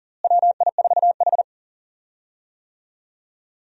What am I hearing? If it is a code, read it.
Morse code, heard as a keyed tone, about 40 wpm, 700 Hz. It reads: WI4H
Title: USA Call Signs - 4 Characters - Repeated 2x - 40wpm